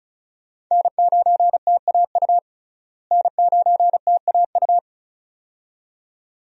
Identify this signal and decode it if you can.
Morse code, heard as a keyed tone, about 36 wpm, 700 Hz. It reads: N9TAU N9TAU
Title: USA Call Signs - 5 Characters - Repeated 2x - 35wpm